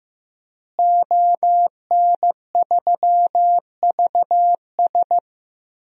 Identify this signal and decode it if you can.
Morse code, heard as a keyed tone, about 15 wpm, 700 Hz. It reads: ON3VS
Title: DXCC-100 Call Signs - 5 Characters - Repeated 3x - 15wpm